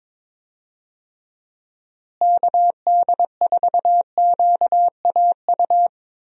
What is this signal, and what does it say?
Morse code, heard as a keyed tone, about 22 wpm, 700 Hz. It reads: KD4QAU
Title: USA Call Signs - 6 Characters - Repeated 3x - 22wpm